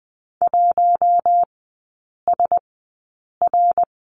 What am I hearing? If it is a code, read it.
Morse code, heard as a keyed tone, about 20 wpm, spaced as for 8 wpm, 700 Hz. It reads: 1SR
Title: 067 Lesson 12e - Letter d - Three Characters 20wpm